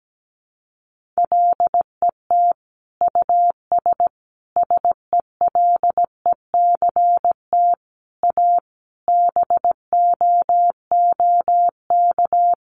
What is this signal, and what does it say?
Morse code, heard as a keyed tone, about 17 wpm, 700 Hz. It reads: LET US SELECT A BOOK
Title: Sentences from Top 1300 Words 17wpm